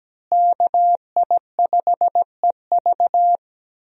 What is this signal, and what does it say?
Morse code, heard as a keyed tone, about 17 wpm, 700 Hz. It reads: KI5EV
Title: USA Call Signs - 5 Characters - Repeated 3x - 17wpm